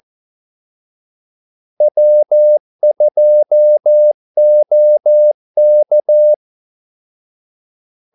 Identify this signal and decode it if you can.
Morse code, heard as a keyed tone, about 14 wpm, 600 Hz. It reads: W2OK
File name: Callsigns-14wpm